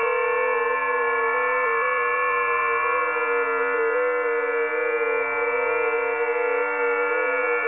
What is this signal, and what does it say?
FT8.
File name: tests / websdr test4